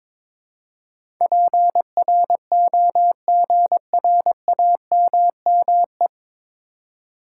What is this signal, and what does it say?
Morse code, heard as a keyed tone, about 22 wpm, 700 Hz. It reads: PROGRAMME
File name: practice / 22wpm / 9 Letter Words - First 3 Letters Given - 22wpm